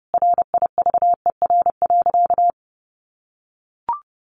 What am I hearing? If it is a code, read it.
Morse code, heard as a keyed tone, about 30 wpm, 700 Hz. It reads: RIVER.
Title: Sentences from Top 1300 Words 30wpm